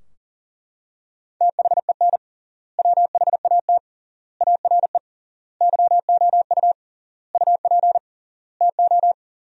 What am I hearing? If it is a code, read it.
Morse code, heard as a keyed tone, about 40 wpm, 700 Hz. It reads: THEN WHAT ARE YOU UP TO
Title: 3x Word-Spacing - Sentences from Top 100 Words - 40wpm